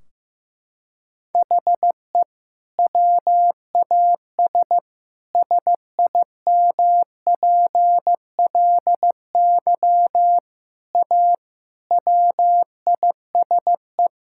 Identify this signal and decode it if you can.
Morse code, heard as a keyed tone, about 15 wpm, 700 Hz. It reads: HE WAS SIMPLY A WISE
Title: Sentences from Top 1600 Words 15wpm